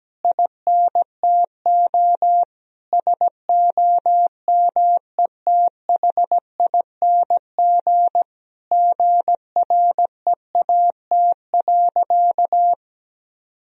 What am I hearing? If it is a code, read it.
Morse code, heard as a keyed tone, about 17 wpm, 700 Hz. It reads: INTO SOMETHING GREAT.
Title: Sentences from Top 1000 Words - Review Part 2 - 17wpm